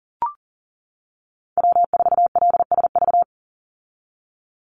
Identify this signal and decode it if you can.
Morse code, heard as a keyed tone, about 40 wpm, 700 Hz. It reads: W4LSV
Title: Call Signs - 40wpm